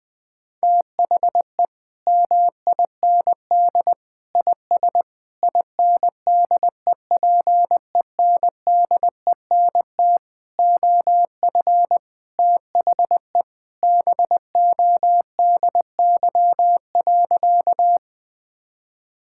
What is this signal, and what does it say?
Morse code, heard as a keyed tone, about 21 wpm, 700 Hz. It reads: THE MIND IS INDEPENDENT OF THE BODY.
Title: Sentences from Top 1000 Words - Review Part 6 - 20wpm